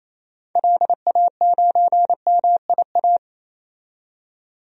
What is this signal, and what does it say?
Morse code, heard as a keyed tone, about 28 wpm, 700 Hz. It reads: LA9MIA
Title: DXCC-100 Call Signs - 6 Characters - 28wpm